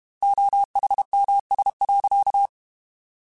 Morse code, heard as a keyed tone, about 32 wpm, 800 Hz. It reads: OHMS.
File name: Technical Subjects / Technical Subjects-mp3s / XL @30